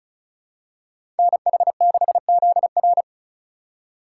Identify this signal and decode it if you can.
Morse code, heard as a keyed tone, about 36 wpm, 700 Hz. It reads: NH6ZR
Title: USA Call Signs - 5 Characters - 2x Character Spacing - 35wpm